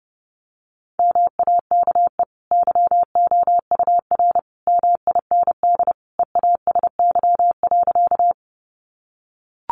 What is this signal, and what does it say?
Morse code, heard as a keyed tone, about 30 wpm, 700 Hz. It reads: MAKE YOUR MIND EASY.
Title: Sentences from Top 700 Words 30wpm